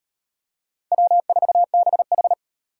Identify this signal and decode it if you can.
Morse code, heard as a keyed tone, about 38 wpm, 700 Hz. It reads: W4BH